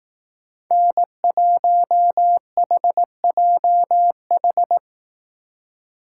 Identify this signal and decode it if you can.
Morse code, heard as a keyed tone, about 18 wpm, 700 Hz. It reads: N1HJH
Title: Speed Racing - USA 1x3 Call Signs 15wpm